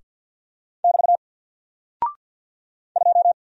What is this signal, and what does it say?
Morse code, heard as a keyed tone, about 48 wpm, 700 Hz. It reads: <BT> 2
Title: Single Character (Letter:Number:Punctuation:Prosign) 50wpm